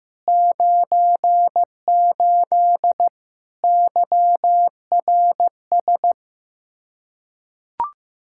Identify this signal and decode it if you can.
Morse code, heard as a keyed tone, about 16 wpm, 700 Hz. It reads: 98 YRS
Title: K1USN SST QSO Practice 15wpm